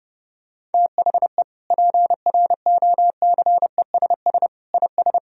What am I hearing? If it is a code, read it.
Morse code, heard as a keyed tone, about 32 wpm, 700 Hz. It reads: THE PROCESS IS
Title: Sentences from Top 1000 Words - Review Part 3 - 30wpm